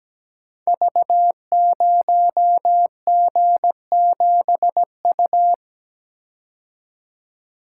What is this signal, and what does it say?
Morse code, heard as a keyed tone, about 17 wpm, 700 Hz. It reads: V0G7U
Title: Code Groups - 5 Letter:Numbers - 17wpm